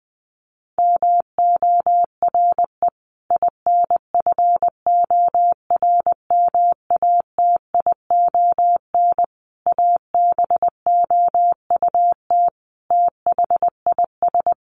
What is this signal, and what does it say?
Morse code, heard as a keyed tone, about 19 wpm, 700 Hz. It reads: MORE INFORMATION ABOUT THIS
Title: Sentences from Top 1200 Words 20wpm